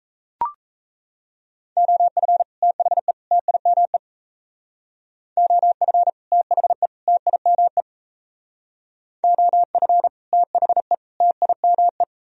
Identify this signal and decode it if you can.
Morse code, heard as a keyed tone, about 35 wpm, 700 Hz. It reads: OF THE TIME OF THE TIME OF THE TIME
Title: Speed-Racing - Sets of 3 Words 28wpm